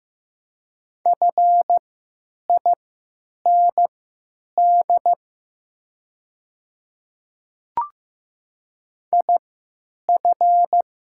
Morse code, heard as a keyed tone, about 15 wpm, spaced as for 9 wpm, 700 Hz. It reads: FIND IF
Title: 3x Character Spacing - Top 100 Words - 15wpm